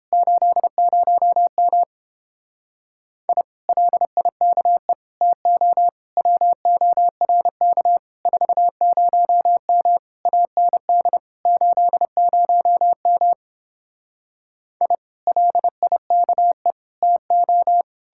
Morse code, heard as a keyed tone, about 31 wpm, 700 Hz. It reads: 80M I LIKE TO WORK 40M AND 80M I LIKE TO